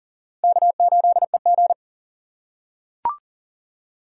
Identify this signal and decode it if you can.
Morse code, heard as a keyed tone, about 40 wpm, 700 Hz. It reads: K8EG
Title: USA Call Signs - 4 Characters - 4x Character Spacing - 40wpm